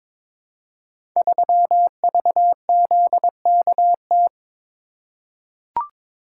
Morse code, heard as a keyed tone, about 22 wpm, 700 Hz. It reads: 3VZKT
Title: Code Groups - 5 Letter:Numbers - 22wpm